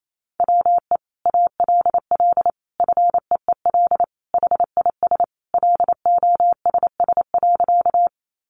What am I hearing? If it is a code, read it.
Morse code, heard as a keyed tone, about 28 wpm, 700 Hz. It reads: WE ALL FEEL HIS LOSS.
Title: Sentences from Top 1400 Words 28wpm